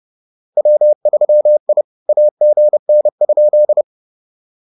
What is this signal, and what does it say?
Morse code, heard as a keyed tone, about 31 wpm, 600 Hz. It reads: W3I AGN?